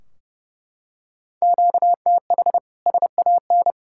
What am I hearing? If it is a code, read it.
Morse code, heard as a keyed tone, about 30 wpm, 700 Hz. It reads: QTH SAN
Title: K1USN SST QSO Practice 30wpm